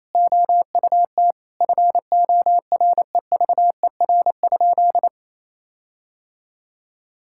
Morse code, heard as a keyed tone, about 27 wpm, 700 Hz. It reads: OUT FOREVER?
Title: Sentences from Top 1600 Words 28wpm